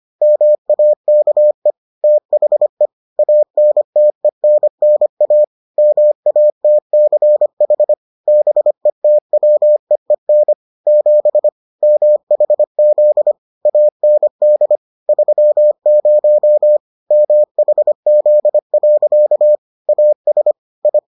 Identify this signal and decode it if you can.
Morse code, heard as a keyed tone, about 25 wpm, 600 Hz. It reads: MAKE THE ANTENNA MATCH BETWEEN 7 MHZ AND 30 MHZ. AS I